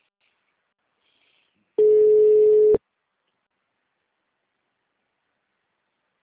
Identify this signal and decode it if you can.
Ringback tone (Europe, 425 Hz).